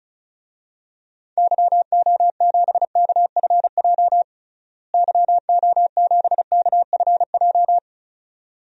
Morse code, heard as a keyed tone, about 36 wpm, 700 Hz. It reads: YO7KFJ YO7KFJ
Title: DXCC-100 Call Signs - 6 Characters - Repeated 2x - 35wpm